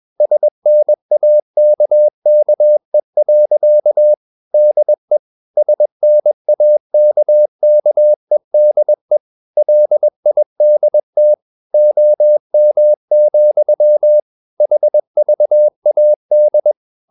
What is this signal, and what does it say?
Morse code, heard as a keyed tone, about 21 wpm, 600 Hz. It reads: SNAKKE. DE SNAKKEDE LIDT OM, HVAD
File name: Ferie QRS 21wpm